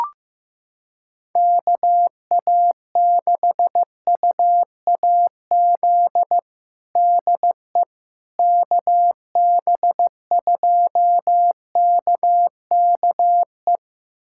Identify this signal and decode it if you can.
Morse code, heard as a keyed tone, about 15 wpm, 700 Hz. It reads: KA6UAZ DE KB2KKE
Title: USA Callsign DE Callsign - 6 Characters - 15wpm